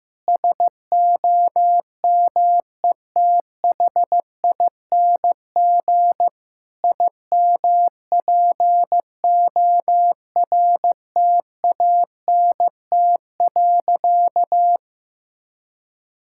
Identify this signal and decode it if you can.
Morse code, heard as a keyed tone, about 15 wpm, 700 Hz. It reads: SOMETHING IMPORTANT.
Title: Sentences from Top 1000 Words - Review Part 6 - 15wpm